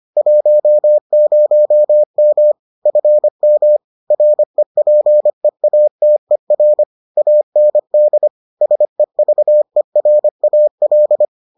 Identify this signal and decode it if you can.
Morse code, heard as a keyed tone, about 25 wpm, 600 Hz. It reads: 10M FM REPEATER AND SEVERAL